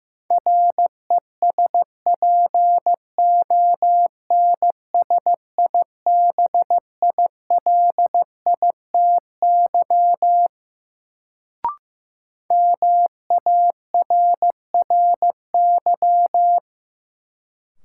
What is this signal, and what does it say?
Morse code, heard as a keyed tone, about 15 wpm, 700 Hz. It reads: RESPONSIBILITY MARRY
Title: Sentences from Top 1200 Words - New Word Drill Alt - 15wpm